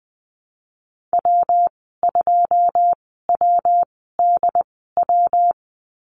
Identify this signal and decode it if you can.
Morse code, heard as a keyed tone, about 20 wpm, spaced as for 15 wpm, 700 Hz. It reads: W2WDW
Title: 122 Lesson 20g - Number 3 - Call Signs 20wpm